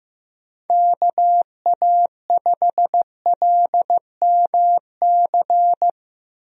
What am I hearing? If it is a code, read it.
Morse code, heard as a keyed tone, about 15 wpm, 700 Hz. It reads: KA5LMC
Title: USA Call Signs - 6 Characters - 15wpm